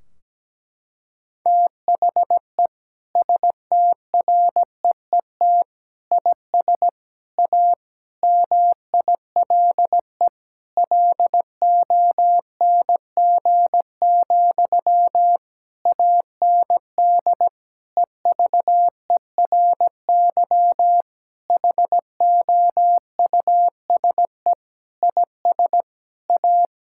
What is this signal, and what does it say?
Morse code, heard as a keyed tone, about 17 wpm, 700 Hz. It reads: THE STREET IS A MILE LONG, AND EVERY HOUSE IS A
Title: Sentences from Top 1600 Words 17wpm